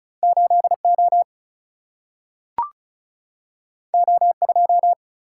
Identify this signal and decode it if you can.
Morse code, heard as a keyed tone, about 36 wpm, 700 Hz. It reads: 8O O2
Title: Single-Letter Single-Number 35wpm